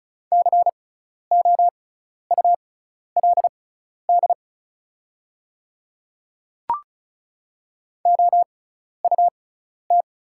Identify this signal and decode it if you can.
Morse code, heard as a keyed tone, about 36 wpm, spaced as for 12 wpm, 700 Hz. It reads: COULD OUT
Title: 6x Character Spacing - Top 200 Words - 35wpm